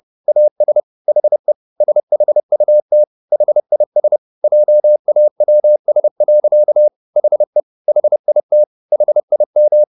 Morse code, heard as a keyed tone, about 30 wpm, 600 Hz. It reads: AS HE SHUT HIS JAWS. HE HIT HIM